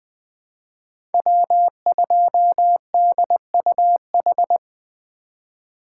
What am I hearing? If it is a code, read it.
Morse code, heard as a keyed tone, about 20 wpm, 700 Hz. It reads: W2DUH
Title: USA Call Signs - 5 Characters - 4x Character Spacing - 20wpm